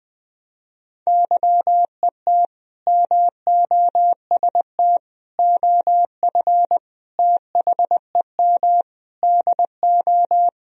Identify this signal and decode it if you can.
Morse code, heard as a keyed tone, about 20 wpm, 700 Hz. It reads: YET MOST OF THEM DO
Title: Sentences from Top 1000 Words - Review Part 1 - 20wpm